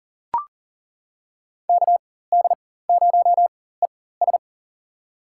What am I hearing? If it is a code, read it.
Morse code, heard as a keyed tone, about 40 wpm, spaced as for 19 wpm, 700 Hz. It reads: KD0ES
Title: USA Call Signs - 5 Characters - 4x Character Spacing - 40wpm